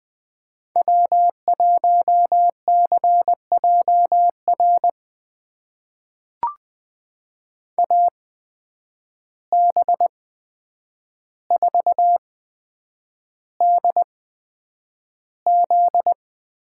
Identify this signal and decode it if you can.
Morse code, heard as a keyed tone, about 20 wpm, spaced as for 9 wpm, 700 Hz. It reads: W1CJR AB4DZ
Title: USA Call Signs - 5 Characters - 8x Character Spacing - 20wpm